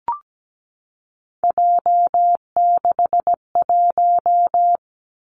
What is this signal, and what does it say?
Morse code, heard as a keyed tone, about 17 wpm, 700 Hz. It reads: J61
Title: Three Challenging Letter-Number Combinations 17wpm